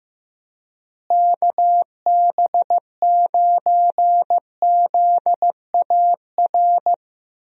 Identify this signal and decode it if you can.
Morse code, heard as a keyed tone, about 15 wpm, 700 Hz. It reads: KB9ZAR
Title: USA Call Signs - 6 Characters - Repeated 3x - 15wpm